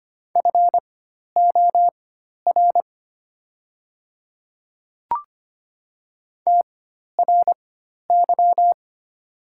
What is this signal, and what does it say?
Morse code, heard as a keyed tone, about 25 wpm, spaced as for 12 wpm, 700 Hz. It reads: FOR TRY
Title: 4x Character Spacing - Top 300 Words - 25wpm